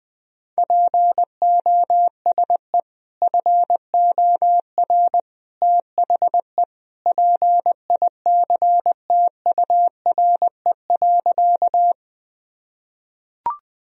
Morse code, heard as a keyed tone, about 20 wpm, 700 Hz. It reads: POSE FOR THE PICTURE.
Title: Sentences from Top 1000 Words - Review Part 2 - 20wpm